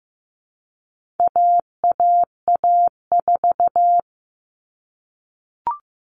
Morse code, heard as a keyed tone, about 15 wpm, 700 Hz. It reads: AAA4
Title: All Cut Numbers - 4-Digit 15wpm